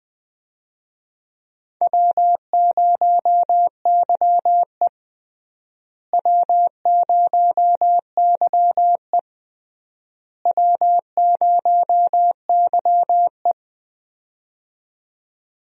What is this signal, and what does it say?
Morse code, heard as a keyed tone, about 21 wpm, 700 Hz. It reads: W0YE W0YE W0YE
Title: USA Call Signs - 4 Characters - Repeated 3x - 20wpm